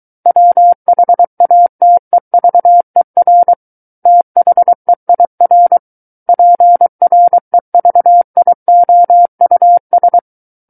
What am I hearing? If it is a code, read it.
Morse code, heard as a keyed tone, about 23 wpm, 700 Hz. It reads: WHATEVER THEIR PREVIOUS
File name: Book One / 14-Chapter 14 - 23 WPM